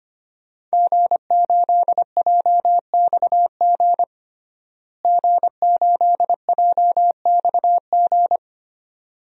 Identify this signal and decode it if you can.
Morse code, heard as a keyed tone, about 25 wpm, 700 Hz. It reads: G8JXG G8JXG
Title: DXCC-100 Call Signs - 5 Characters - Repeated 2x - 25wpm